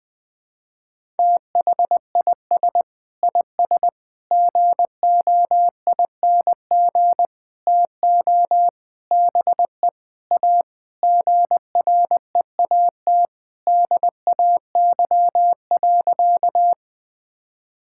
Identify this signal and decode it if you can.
Morse code, heard as a keyed tone, about 21 wpm, 700 Hz. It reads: THIS IS GOING TO BE A GREAT DAY.
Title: Sentences with 6-Word Phrases - Spoken Beginning - 20wpm